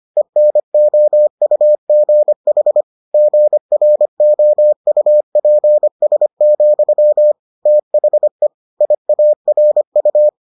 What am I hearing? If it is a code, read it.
Morse code, heard as a keyed tone, about 25 wpm, 600 Hz. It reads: ENOUGH GROUPS, THE IARU